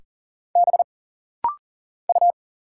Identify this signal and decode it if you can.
Morse code, heard as a keyed tone, about 40 wpm, 700 Hz. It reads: B U
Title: Single Letters - Rapid-Fire 40wpm